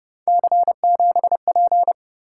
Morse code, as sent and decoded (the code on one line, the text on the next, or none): -.-. --... .--.
C7P